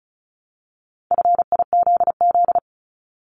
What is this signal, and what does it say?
Morse code, heard as a keyed tone, about 35 wpm, 700 Hz. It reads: FIZZ
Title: Multi-letter Phonemes 35wpm